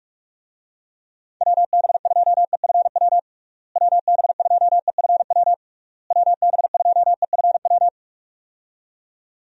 Morse code, heard as a keyed tone, about 46 wpm, 700 Hz. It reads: WB2EFW WB2EFW WB2EFW
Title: USA Call Signs - 6 Characters - Repeated 3x - 45wpm